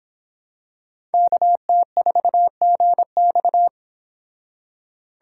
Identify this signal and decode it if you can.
Morse code, heard as a keyed tone, about 26 wpm, 700 Hz. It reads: KT4GX